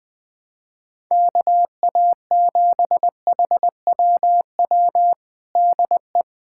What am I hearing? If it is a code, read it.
Morse code, heard as a keyed tone, about 21 wpm, 700 Hz. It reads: KA7HWW DE
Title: USA Callsign DE Callsign 20wpm